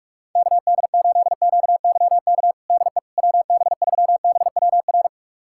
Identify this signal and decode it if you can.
Morse code, heard as a keyed tone, about 46 wpm, 690 Hz. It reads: KD8QYK DE WB3BWR
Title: USA Callsign DE Callsign 45wpm